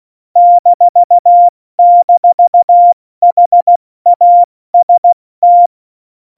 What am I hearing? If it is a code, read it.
Morse code, heard as a keyed tone, about 16 wpm, 700 Hz. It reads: --HAST
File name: Morsekurs / Witze / Witz 18. September 2018, 70 F80